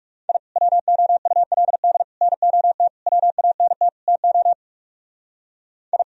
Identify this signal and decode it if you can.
Morse code, heard as a keyed tone, about 36 wpm, 700 Hz. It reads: I WOULD NOT WANT TO I
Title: Speed-Racing - Sets of 5 Words 30wpm